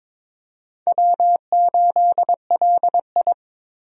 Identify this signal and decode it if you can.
Morse code, heard as a keyed tone, about 22 wpm, 700 Hz. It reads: W8LI
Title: USA Call Signs - 4 Characters - 22wpm